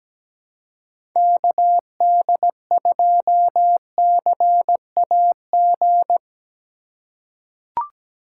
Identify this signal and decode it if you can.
Morse code, heard as a keyed tone, about 17 wpm, 700 Hz. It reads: KD2CAG
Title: USA Call Signs - 6 Characters - 2x Character Spacing - 17wpm